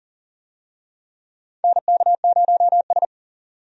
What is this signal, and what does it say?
Morse code, heard as a keyed tone, about 40 wpm, 700 Hz. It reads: NK0S
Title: USA Call Signs - 4 Characters - 40wpm